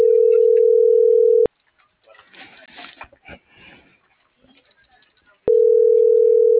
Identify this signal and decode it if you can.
Ringback tone (North America), 440 + 480 Hz.